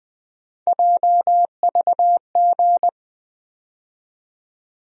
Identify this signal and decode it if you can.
Morse code, heard as a keyed tone, about 21 wpm, 700 Hz. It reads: JVG
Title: USA Call Sign Postfixes - 3 Letters 20wpm